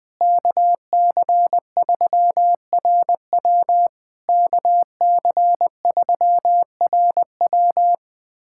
Morse code, heard as a keyed tone, about 20 wpm, 700 Hz. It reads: KC3RW KC3RW
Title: POTA QSO Practice - 20wpm